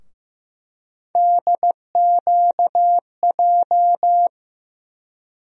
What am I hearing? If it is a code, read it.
Morse code, heard as a keyed tone, about 15 wpm, 700 Hz. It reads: DQJ
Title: USA Call Sign Postfixes - 3 Letters 15wpm